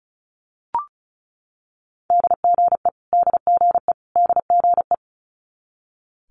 Morse code, heard as a keyed tone, about 36 wpm, 700 Hz. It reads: DGE DGE DGE
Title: Multi-letter Phonemes Repeated 3x 35wpm